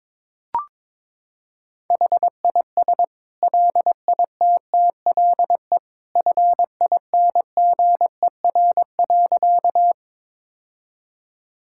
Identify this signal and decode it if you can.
Morse code, heard as a keyed tone, about 22 wpm, 700 Hz. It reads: HIS LITTLE FINGER.
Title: Sentences from Top 1600 Words 22wpm